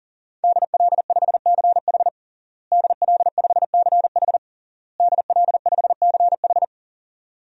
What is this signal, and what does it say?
Morse code, heard as a keyed tone, about 38 wpm, 700 Hz. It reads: DL5CH DL5CH DL5CH
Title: DXCC-100 Call Signs - 5 Characters - Repeated 3x - 40wpm